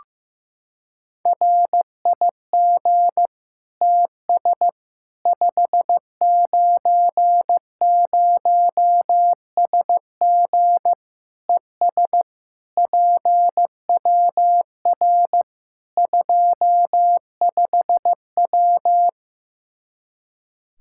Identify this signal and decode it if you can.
Morse code, heard as a keyed tone, about 15 wpm, 700 Hz. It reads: RIG TS 590SG ES PWR 25W
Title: Practice QSO - Exchange 2 - Info - 15wpm